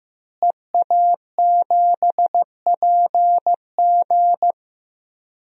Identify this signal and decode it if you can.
Morse code, heard as a keyed tone, about 15 wpm, 700 Hz. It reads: EA7PG
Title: DXCC-100 Call Signs - 5 Characters - Repeated 3x - 15wpm